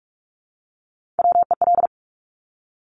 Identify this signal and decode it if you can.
Morse code, heard as a keyed tone, about 46 wpm, 700 Hz. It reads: WEL
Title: Top Three Letter Combinations in Words 45wpm